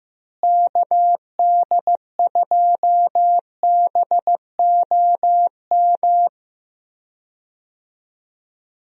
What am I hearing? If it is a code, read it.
Morse code, heard as a keyed tone, about 15 wpm, 700 Hz. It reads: KD2BOM
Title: USA Call Signs - 6 Characters - Repeated 3x - 15wpm